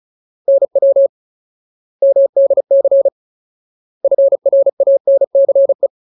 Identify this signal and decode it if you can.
Morse code, heard as a keyed tone, about 34 wpm, 550 Hz. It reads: NW MDC FRANCE
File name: Ultimate Word List / 35WPM Ultimate